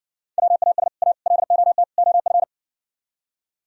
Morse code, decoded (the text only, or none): WAS A LOT OF